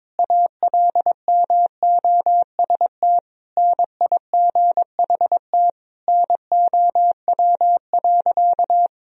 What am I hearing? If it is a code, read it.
Morse code, heard as a keyed tone, about 22 wpm, 700 Hz. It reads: ALMOST NIGHT NOW.
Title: Sentences from Top 1000 Words - Review Part 4 - 22wpm